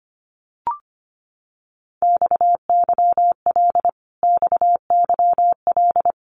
Morse code, heard as a keyed tone, about 25 wpm, 700 Hz. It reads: XYL XYL
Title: Abbreviations Repeated 3x - 25wpm